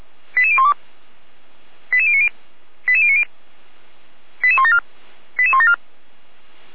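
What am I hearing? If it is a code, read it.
Five-tone selective call (ZVEI), heard as quick runs of five tones, tones of about 70 ms. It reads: 88012 88090 88090 88174 88174
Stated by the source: ZVEI five-tone selective call received by SDR, ID 88169, channel 455.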